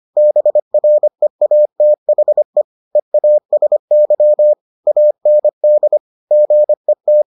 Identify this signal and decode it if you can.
Morse code, heard as a keyed tone, about 25 wpm, 600 Hz. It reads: BREATHE EASY AND GET